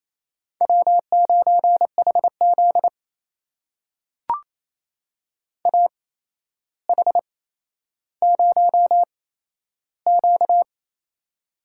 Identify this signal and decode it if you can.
Morse code, heard as a keyed tone, about 27 wpm, spaced as for 12 wpm, 700 Hz. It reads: W9HZ AH0Q
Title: USA Call Signs - 4 Characters - 8x Character Spacing - 28wpm